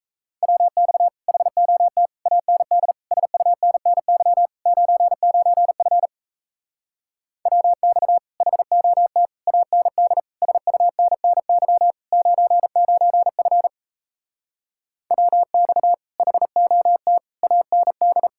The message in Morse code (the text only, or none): WX HOT AND SUNNY 99F WX HOT AND SUNNY 99F WX HOT AND